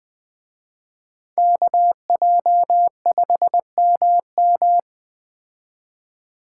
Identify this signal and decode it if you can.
Morse code, heard as a keyed tone, about 20 wpm, 700 Hz. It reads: KJ5MM
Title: Speed Racing - USA 2x2 Call Signs 17wpm